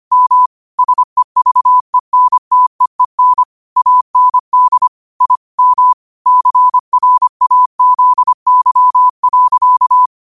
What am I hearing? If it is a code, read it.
Morse code, heard as a keyed tone, about 25 wpm, 1000 Hz. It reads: M SEVENTEEN AND I M CRAZY.
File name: qotdfiles / 25wpm